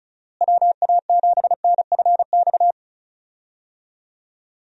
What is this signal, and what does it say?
Morse code, heard as a keyed tone, about 34 wpm, 700 Hz. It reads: WA7NFX
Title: USA Call Signs - 6 Characters - Repeated 2x - 35wpm